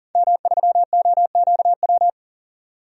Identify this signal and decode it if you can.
Morse code, heard as a keyed tone, about 40 wpm, 700 Hz. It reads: M3OQW
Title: DXCC-100 Call Signs - 5 Characters - 40wpm